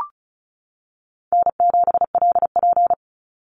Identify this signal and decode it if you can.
Morse code, heard as a keyed tone, about 34 wpm, 700 Hz. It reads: N7LP
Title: NA SOTA Chaser Callsigns 35wpm